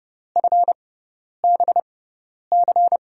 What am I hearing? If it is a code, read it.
Morse code, heard as a keyed tone, about 30 wpm, spaced as for 10 wpm, 700 Hz. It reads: FBC
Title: USA Call Signs - 5 Characters - 6x Character Spacing - 30wpm